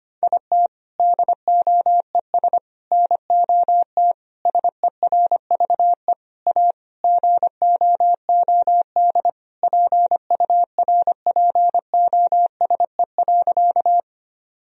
Morse code, heard as a keyed tone, about 25 wpm, 700 Hz. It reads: IT DOES NOT SERVE A GOOD PURPOSE.
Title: Sentences from Top 1000 Words - Review Part 2 - 25wpm